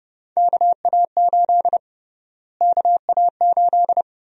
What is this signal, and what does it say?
Morse code, heard as a keyed tone, about 30 wpm, 700 Hz. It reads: KA8 KA8
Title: USA Call Sign Prefixes - Repeated 2x - 30wpm